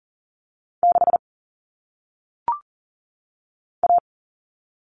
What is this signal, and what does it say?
Morse code, heard as a keyed tone, about 40 wpm, 700 Hz. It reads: 6 A
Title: All Cut Numbers - 1-Digit 40wpm